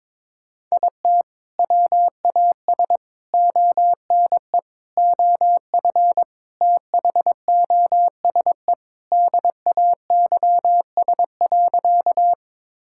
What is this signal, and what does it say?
Morse code, heard as a keyed tone, about 21 wpm, 700 Hz. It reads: IT WAS ONE OF THOSE DAYS.
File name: practice / 22wpm / Sentences with 3-Word Phrases - 22wpm